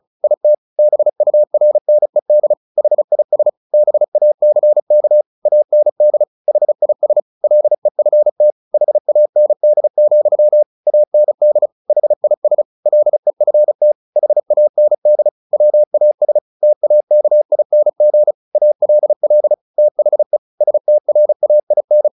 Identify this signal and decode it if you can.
Morse code, heard as a keyed tone, about 34 wpm, 600 Hz. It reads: IT BURNED HIS BACK AND HIS LEFT HAND, AND HIS LEFT HAND WAS TAKING ALL THE STRAIN
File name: the old man and the sea 2 35wpm